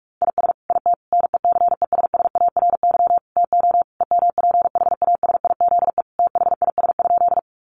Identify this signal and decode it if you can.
Morse code, heard as a keyed tone, about 44 wpm, 700 Hz. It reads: IS IT NECESSARY TO EMPHASIZE THIS?
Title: Sentences from Top 1500 Words 45wpm